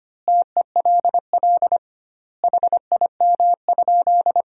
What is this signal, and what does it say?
Morse code, heard as a keyed tone, about 25 wpm, 700 Hz. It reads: TELL HIM?
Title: 2x Word-Spacing - Sentences from Top 200 Words - 25wpm